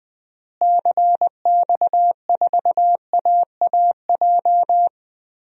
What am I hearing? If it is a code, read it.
Morse code, heard as a keyed tone, about 20 wpm, 700 Hz. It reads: CX4AAJ
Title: DXCC-100 Call Signs - 6 Characters - Repeated 3x - 20wpm